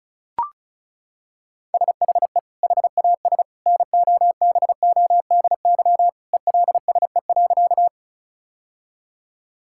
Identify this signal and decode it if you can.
Morse code, heard as a keyed tone, about 34 wpm, 700 Hz. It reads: SHE HAS NOBODY ELSE.
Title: Sentences from Top 1600 Words 35wpm